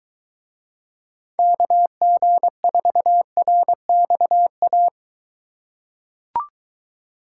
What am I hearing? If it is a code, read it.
Morse code, heard as a keyed tone, about 22 wpm, 700 Hz. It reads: KG4RXA